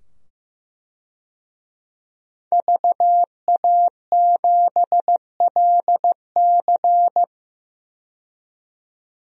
Morse code, heard as a keyed tone, about 15 wpm, 700 Hz. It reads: VA7LC
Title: DXCC-100 Call Signs - 5 Characters - 15wpm